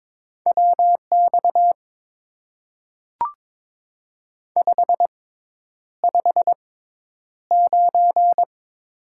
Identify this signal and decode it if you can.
Morse code, heard as a keyed tone, about 22 wpm, spaced as for 10 wpm, 700 Hz. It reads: WX 559
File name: Top 90 QSO Elements - 6x Character Spacing - 22wpm